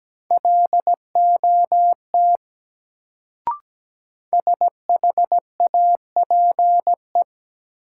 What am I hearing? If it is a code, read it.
Morse code, heard as a keyed tone, about 17 wpm, 700 Hz. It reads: LOT SHAPE
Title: Sentences from Top 500 Words - New Word Drill - 17wpm